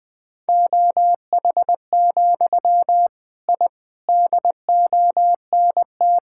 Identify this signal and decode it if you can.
Morse code, heard as a keyed tone, about 21 wpm, 700 Hz. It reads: OH, I DONT
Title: Sentences from Top 1000 Words - Review Part 6 - 20wpm